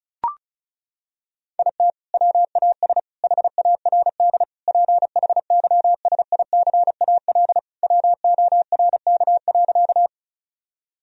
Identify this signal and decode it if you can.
Morse code, heard as a keyed tone, about 36 wpm, 700 Hz. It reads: IT WAS HARD PHYSICAL WORK.
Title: Sentences from Top 1000 Words - Review Part 4 - 35wpm